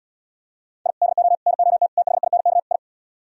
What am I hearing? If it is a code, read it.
Morse code, heard as a keyed tone, about 77 wpm, 700 Hz. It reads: I DO NOT THINK I